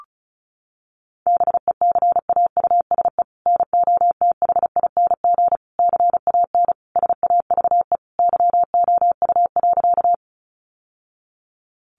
Morse code, heard as a keyed tone, about 34 wpm, 700 Hz. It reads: BECAUSE NOTHING CAN SAVE YOU.
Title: Sentences from Top 1100 Words 35wpm